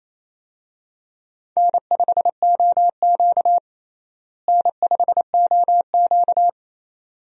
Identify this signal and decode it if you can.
Morse code, heard as a keyed tone, about 28 wpm, 700 Hz. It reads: N5OQ N5OQ
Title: USA Call Signs - 4 Characters - Repeated 2x - 28wpm